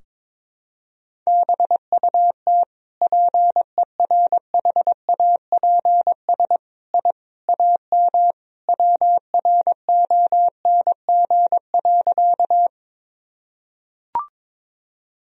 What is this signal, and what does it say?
Morse code, heard as a keyed tone, about 22 wpm, 700 Hz. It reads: BUT PERHAPS I AM WRONG.
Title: Sentences from Top 1000 Words - Review Part 2 - 22wpm